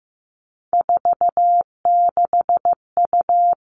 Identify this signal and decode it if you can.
Morse code, heard as a keyed tone, about 15 wpm, 700 Hz. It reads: 46U
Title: All Cut Numbers - 3-Digit 15wpm